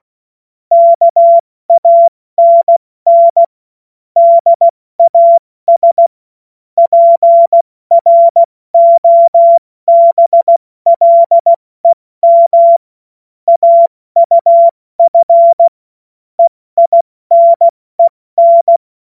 Morse code, heard as a keyed tone, about 16 wpm, spaced as for 14 wpm, 680 Hz. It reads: KANN DAS PROBLEM AUF EINEN
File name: Morsekurs / Witze / Witz 21. Juni 2022 70 F80